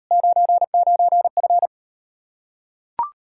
Morse code, heard as a keyed tone, about 39 wpm, 700 Hz. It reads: 99F